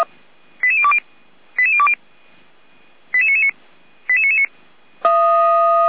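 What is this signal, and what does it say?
Five-tone selective call (ZVEI), heard as quick runs of five tones, tones of about 70 ms. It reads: 88020 88020 88999 88999
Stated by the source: Five-tone ZVEI selective call received on SDR, fire brigade Orlamünde (feuerwehr-orlamuende).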